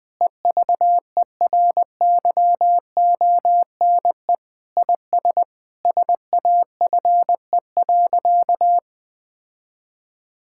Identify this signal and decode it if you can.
Morse code, heard as a keyed tone, about 20 wpm, 700 Hz. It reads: EVERYONE IS SAFE.
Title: Sentences with 5-Word Phrases - 20wpm